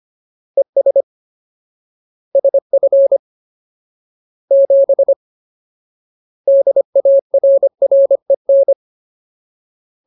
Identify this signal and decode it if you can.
Morse code, heard as a keyed tone, about 25 wpm, 550 Hz. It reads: ES SF 7 DARREN
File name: Ultimate Word List / 25WPM Ultimate